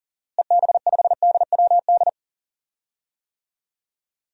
Morse code, heard as a keyed tone, about 40 wpm, 700 Hz. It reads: EB5DWD
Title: DXCC-100 Call Signs - 6 Characters - Repeated 3x - 40wpm